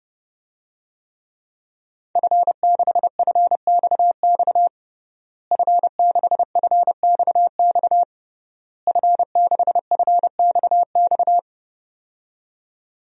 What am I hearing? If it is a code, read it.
Morse code, heard as a keyed tone, about 30 wpm, 700 Hz. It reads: F6FXX F6FXX F6FXX
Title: DXCC-100 Call Signs - 5 Characters - Repeated 3x - 30wpm